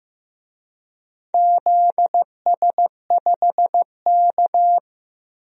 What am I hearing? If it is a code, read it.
Morse code, heard as a keyed tone, about 15 wpm, 700 Hz. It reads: ZS5K
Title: DXCC-100 Call Signs - 4 Characters - 15wpm